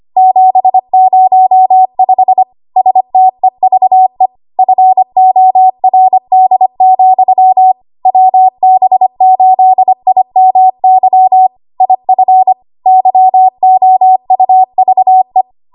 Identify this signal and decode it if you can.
Morse code, heard as a keyed tone, about 25 wpm, 750 Hz. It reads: 705 STEVE FORD, WB8IMY IF YOUVE
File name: Archive / 25 WPM / 25WPM